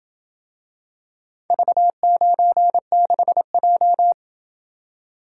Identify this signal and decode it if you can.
Morse code, heard as a keyed tone, about 27 wpm, 700 Hz. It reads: V96J